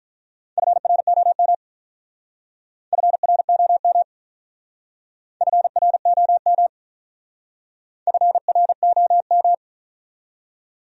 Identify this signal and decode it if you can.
Morse code, heard as a keyed tone, about 42 wpm, 700 Hz. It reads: FROM FROM FROM FROM
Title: Speed-Racing Top 100 Words 35wpm